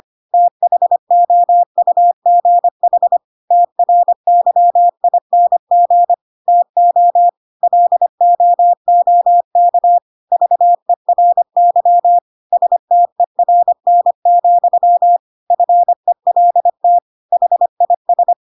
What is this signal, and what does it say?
Morse code, heard as a keyed tone, about 25 wpm, 700 Hz. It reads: THOUGH TRYING TO LOOK VERY STERN, FELT HIS